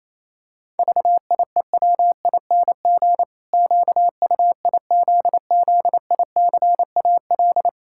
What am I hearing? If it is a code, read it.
Morse code, heard as a keyed tone, about 27 wpm, 700 Hz. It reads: VIEWING QUIZZICAL
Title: Pangram Challenge v2 - 28wpm